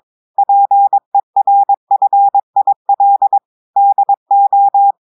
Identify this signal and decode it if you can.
Morse code, heard as a keyed tone, about 22 wpm, 800 Hz. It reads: PERFIL DO
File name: audios / portugais / portugais texte en clair 2 vitesse 22 mots minutes